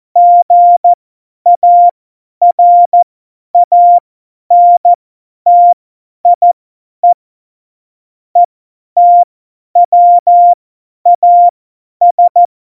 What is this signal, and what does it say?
Morse code, heard as a keyed tone, about 14 wpm, spaced as for 10 wpm, 700 Hz. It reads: GARANTIE ETWAS
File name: Morsekurs / Witze / Witz 05 Mai 2020, 50 F70